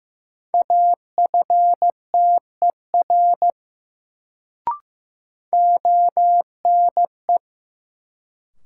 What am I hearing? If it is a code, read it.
Morse code, heard as a keyed tone, about 15 wpm, 700 Hz. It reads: AFTER ONE
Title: Sets of 2 Words - New Word Drill - 15wpm